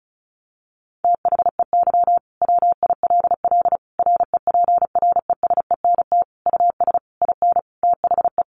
Morse code, heard as a keyed tone, about 34 wpm, 700 Hz. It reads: THEY WILL REPRESENT US IN THE
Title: Sentences from Top 1200 Words 35wpm